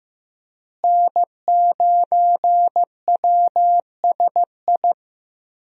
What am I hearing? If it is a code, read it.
Morse code, heard as a keyed tone, about 15 wpm, 700 Hz. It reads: N9WSI
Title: USA Call Signs - 5 Characters - Repeated 3x - 15wpm